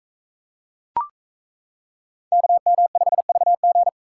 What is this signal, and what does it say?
Morse code, heard as a keyed tone, about 40 wpm, 700 Hz. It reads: KM5VG